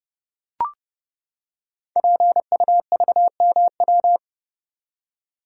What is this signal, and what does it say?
Morse code, heard as a keyed tone, about 30 wpm, 700 Hz. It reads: PUVMW
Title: Code Groups - 5 Letters - 30wpm